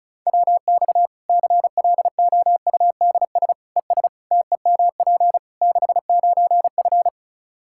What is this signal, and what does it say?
Morse code, heard as a keyed tone, about 36 wpm, 700 Hz. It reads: WX CLOUDS ES TEMP 69F
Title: K1USN SST QSO Practice 35wpm